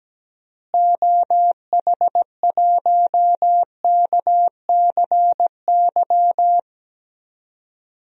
Morse code, heard as a keyed tone, about 17 wpm, 700 Hz. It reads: OH1KCY
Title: DXCC-100 Call Signs - 6 Characters - Repeated 3x - 17wpm